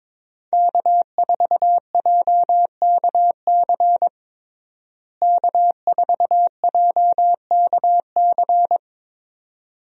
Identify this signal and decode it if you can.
Morse code, heard as a keyed tone, about 21 wpm, 700 Hz. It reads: K4JKC K4JKC
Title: USA Call Signs - 5 Characters - Repeated 3x - 22wpm